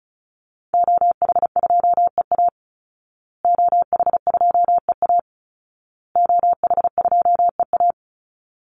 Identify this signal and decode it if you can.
Morse code, heard as a keyed tone, about 34 wpm, 700 Hz. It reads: OH2EA OH2EA OH2EA
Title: Call Signs Repeated 2x - 35wpm